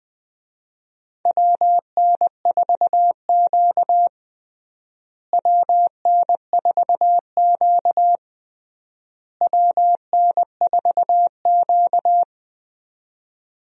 Morse code, heard as a keyed tone, about 20 wpm, 700 Hz. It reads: WN4Q WN4Q WN4Q
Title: USA Call Signs - 4 Characters - Repeated 3x - 20wpm